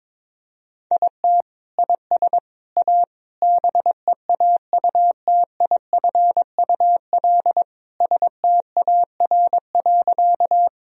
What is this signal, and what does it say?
Morse code, heard as a keyed tone, about 21 wpm, 700 Hz. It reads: IT IS A BEAUTIFUL STAR.
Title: Sentences from Top 1000 Words - Review Part 2 - 22wpm